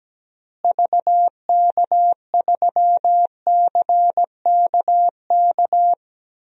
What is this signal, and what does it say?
Morse code, heard as a keyed tone, about 17 wpm, 700 Hz. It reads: VK3CKK
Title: DXCC-100 Call Signs - 6 Characters - Repeated 3x - 17wpm